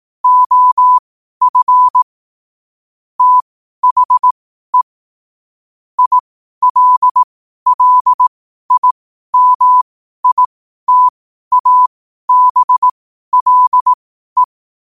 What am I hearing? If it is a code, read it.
Morse code, heard as a keyed tone, about 18 wpm, spaced as for 12 wpm, 1000 Hz. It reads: OF THE ILLIMITABLE
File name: qotdfiles / 12wpm